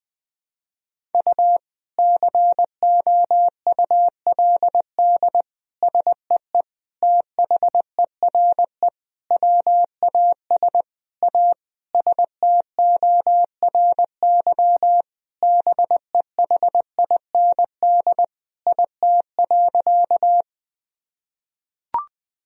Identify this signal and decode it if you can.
Morse code, heard as a keyed tone, about 20 wpm, 700 Hz. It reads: U COULD SEE THERE WAS A STORY BEHIND IT.
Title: Sentences from Top 500 Words and Abbreviations 20wpm